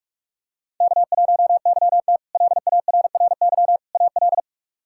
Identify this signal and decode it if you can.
Morse code, heard as a keyed tone, about 44 wpm, 700 Hz. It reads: K1YT LARRY AL
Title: K1USN SST QSO Practice - 45wpm